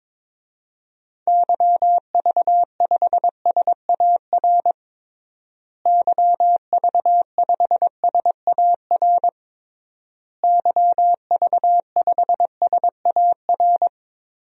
Morse code, heard as a keyed tone, about 22 wpm, 700 Hz. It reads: YV5SAR YV5SAR YV5SAR
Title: DXCC-100 Call Signs - 6 Characters - Repeated 3x - 22wpm